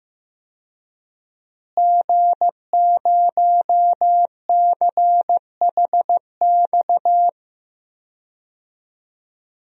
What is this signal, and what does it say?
Morse code, heard as a keyed tone, about 15 wpm, 700 Hz. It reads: G0CHX
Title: DXCC-100 Call Signs - 5 Characters - Repeated 2x - 15wpm